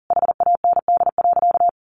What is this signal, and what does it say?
Morse code, heard as a keyed tone, about 40 wpm, 700 Hz. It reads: HAND.
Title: Sentences from Top 1100 Words 40wpm